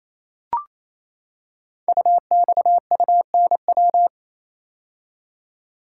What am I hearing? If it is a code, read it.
Morse code, heard as a keyed tone, about 27 wpm, 700 Hz. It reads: UXUNW
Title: Code Groups - 5 Letters - 28wpm